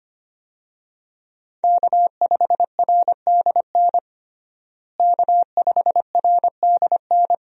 Morse code, heard as a keyed tone, about 25 wpm, 700 Hz. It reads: K5RDN K5RDN
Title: USA Call Signs - 5 Characters - Repeated 2x - 25wpm